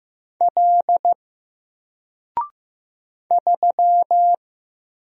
Morse code, heard as a keyed tone, about 15 wpm, 700 Hz. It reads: L 3
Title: Single Character (Letter:Number:Punctuation:Prosign) 15wpm